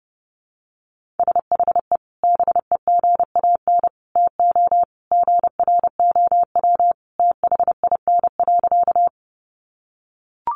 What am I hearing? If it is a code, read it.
Morse code, heard as a keyed tone, about 30 wpm, 700 Hz. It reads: SHE BEGAN TO GROW THIN.
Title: Sentences from Top 1200 Words 30wpm